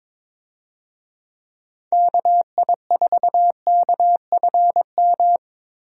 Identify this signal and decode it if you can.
Morse code, heard as a keyed tone, about 21 wpm, 700 Hz. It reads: KI4KFM
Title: USA Call Signs - 6 Characters - Repeated 3x - 22wpm